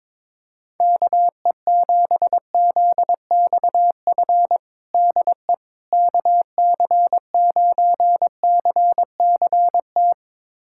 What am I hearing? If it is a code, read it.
Morse code, heard as a keyed tone, about 22 wpm, 700 Hz. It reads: KE7ZXF DE KC9CCT
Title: USA Callsign DE Callsign - 6 Characters - 22wpm